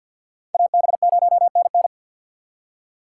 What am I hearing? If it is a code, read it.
Morse code, heard as a keyed tone, about 62 wpm, 700 Hz. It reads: AB0NN
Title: Speed Racing - USA 2x2 Call Signs 50wpm